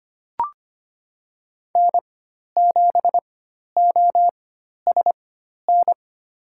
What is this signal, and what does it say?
Morse code, heard as a keyed tone, about 25 wpm, spaced as for 12 wpm, 700 Hz. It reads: N7OSN
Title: USA Call Signs - 4x Character Spacing - 25wpm